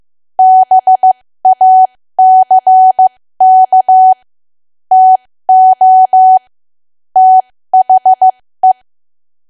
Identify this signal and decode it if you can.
Morse code, heard as a keyed tone, about 15 wpm, spaced as for 13 wpm, 750 Hz. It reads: BACK TO THE
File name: cwa / W1AW-Level2 / 13WPM